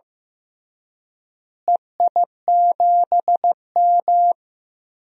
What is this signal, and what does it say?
Morse code, heard as a keyed tone, about 15 wpm, 700 Hz. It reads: EI7M
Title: DXCC-100 Call Signs - 4 Characters - Repeated 3x - 15wpm